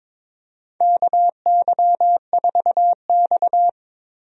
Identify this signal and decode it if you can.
Morse code, heard as a keyed tone, about 22 wpm, 700 Hz. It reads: KY4X
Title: USA Call Signs - 4 Characters - Repeated 3x - 22wpm